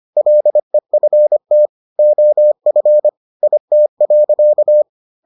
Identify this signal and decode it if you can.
Morse code, heard as a keyed tone, about 25 wpm, 600 Hz. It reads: LEFT OF IT.